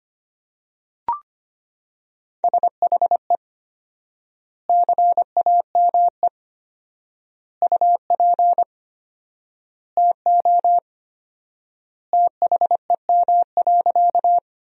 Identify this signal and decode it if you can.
Morse code, heard as a keyed tone, about 25 wpm, 700 Hz. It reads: SHE CAME UP TO THEM.
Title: 4x Word-Spacing - Sentences from Top 200 Words - 25wpm